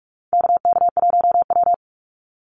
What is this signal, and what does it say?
Morse code, heard as a keyed tone, about 44 wpm, 700 Hz. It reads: KK1W
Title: NA SOTA Activator Callsigns 45wpm